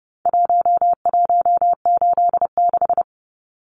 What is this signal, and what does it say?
Morse code, heard as a keyed tone, about 30 wpm, 700 Hz. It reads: 1186
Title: Numbers - Contest Style - 4-Digit 30wpm